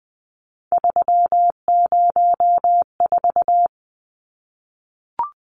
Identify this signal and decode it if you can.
Morse code, heard as a keyed tone, about 20 wpm, 700 Hz. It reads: 304
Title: Numbers - 3-Digit 20wpm